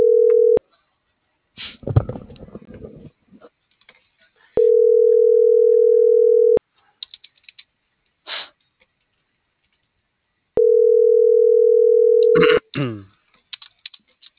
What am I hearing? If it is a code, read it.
Ringback tone (North America), 440 + 480 Hz.